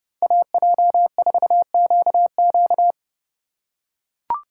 Morse code, heard as a keyed tone, about 29 wpm, 700 Hz. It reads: AJ4QQ